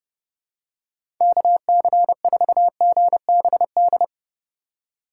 Morse code, heard as a keyed tone, about 32 wpm, 700 Hz. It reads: KC4GBD
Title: USA Call Signs - 6 Characters - 2x Character Spacing - 30wpm